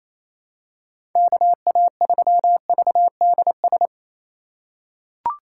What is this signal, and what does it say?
Morse code, heard as a keyed tone, about 28 wpm, 700 Hz. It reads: KA3VDS
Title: USA Call Signs - 4x Character Spacing - 28wpm